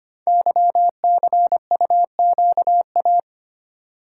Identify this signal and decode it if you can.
Morse code, heard as a keyed tone, about 25 wpm, 700 Hz. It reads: YCUQA
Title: Code Groups - 5 Letters - 25wpm